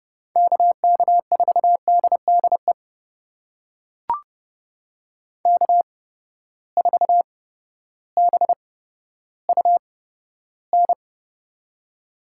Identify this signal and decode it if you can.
Morse code, heard as a keyed tone, about 30 wpm, spaced as for 14 wpm, 700 Hz. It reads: KK4DDE K4BUN
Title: USA Call Signs - 8x Character Spacing - 30wpm